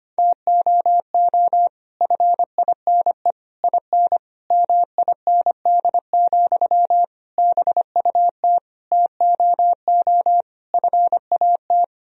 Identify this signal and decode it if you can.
Morse code, heard as a keyed tone, about 25 wpm, 700 Hz. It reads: TOO FINE IN MIND, BUT TOO FAT
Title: Sentences from Top 1000 Words - Review Part 4 - 25wpm